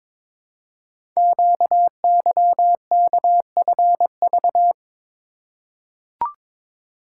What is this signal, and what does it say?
Morse code, heard as a keyed tone, about 22 wpm, 700 Hz. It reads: QYKFV
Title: Code Groups - 5 Letters - 22wpm